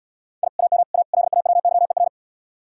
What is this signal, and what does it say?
Morse code, heard as a keyed tone, about 76 wpm, 700 Hz. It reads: I AM A LAWYER